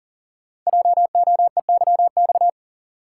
Morse code, heard as a keyed tone, about 42 wpm, 700 Hz. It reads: JOEYX
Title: Code Groups - 5 Letter:Numbers - 40wpm